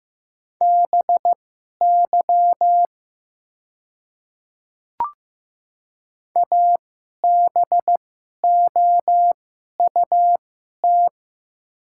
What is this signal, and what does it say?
Morse code, heard as a keyed tone, about 15 wpm, spaced as for 11 wpm, 700 Hz. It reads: BY ABOUT
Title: 2x Character Spacing - Top 100 Words - 15wpm